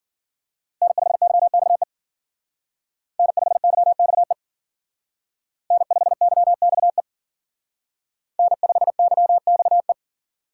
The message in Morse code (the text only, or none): N5YXE N5YXE N5YXE N5YXE